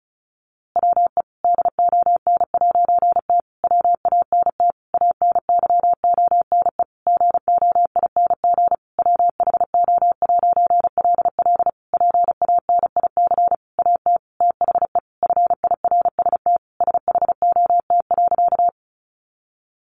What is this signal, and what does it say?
Morse code, heard as a keyed tone, about 34 wpm, 700 Hz. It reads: WE DON'T WANT ANYONE GOING WHO'LL PANIC AT THE FIRST SHOT.
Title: A Sound Of Thunder 35wpm